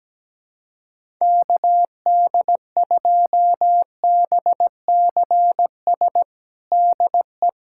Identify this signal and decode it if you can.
Morse code, heard as a keyed tone, about 17 wpm, 700 Hz. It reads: KD2BCS DE
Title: USA Callsign DE Callsign 17wpm